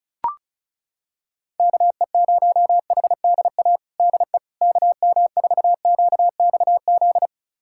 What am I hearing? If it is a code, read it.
Morse code, heard as a keyed tone, about 36 wpm, 700 Hz. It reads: KE0HDA DE KM4QXZ
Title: USA Callsign DE Callsign - 6 Characters - 35wpm